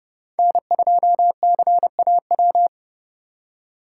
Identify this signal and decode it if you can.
Morse code, heard as a keyed tone, about 30 wpm, 700 Hz. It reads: N2CAW